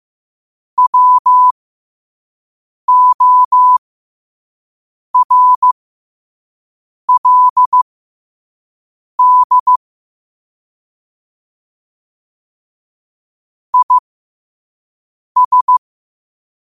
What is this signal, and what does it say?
Morse code, heard as a keyed tone, about 15 wpm, spaced as for 5 wpm, 1000 Hz. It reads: WORLD IS
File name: 5wpm